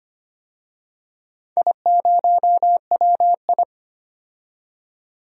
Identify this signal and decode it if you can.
Morse code, heard as a keyed tone, about 25 wpm, 700 Hz. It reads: I0WI
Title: DXCC-100 Call Signs - 4 Characters - 25wpm